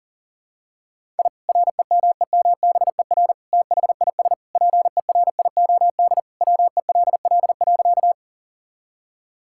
Morse code, heard as a keyed tone, about 37 wpm, 700 Hz. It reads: I REMEMBER THIS PERIOD WELL.
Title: Sentences from Top 1000 Words - Review Part 5 - 40wpm